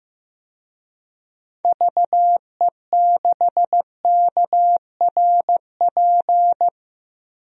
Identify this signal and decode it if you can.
Morse code, heard as a keyed tone, about 15 wpm, 700 Hz. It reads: VE6KRP
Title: DXCC-100 Call Signs - 6 Characters - 15wpm